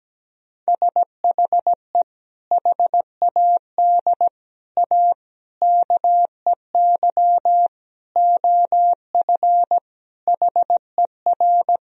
Morse code, heard as a keyed tone, about 17 wpm, 700 Hz. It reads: SHE HAD A KEY OF HER
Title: Sentences from Top 1000 Words - Review Part 4 - 17wpm